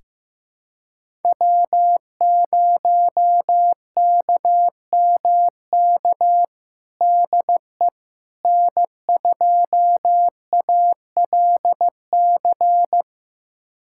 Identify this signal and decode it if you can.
Morse code, heard as a keyed tone, about 16 wpm, 700 Hz. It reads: W0KMK DE N2ALC
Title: USA Callsign DE Callsign - 5 Characters - 15wpm